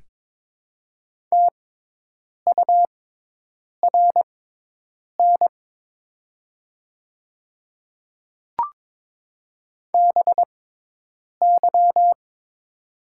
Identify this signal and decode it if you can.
Morse code, heard as a keyed tone, about 22 wpm, spaced as for 8 wpm, 700 Hz. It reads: TURN BY
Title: 6x Character Spacing - Top 200 Words - 22wpm